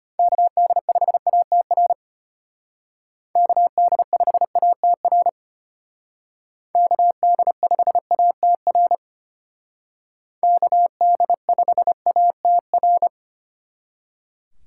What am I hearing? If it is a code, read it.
Morse code, heard as a keyed tone, about 33 wpm, 700 Hz. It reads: KD5ATR KD5ATR KD5ATR KD5ATR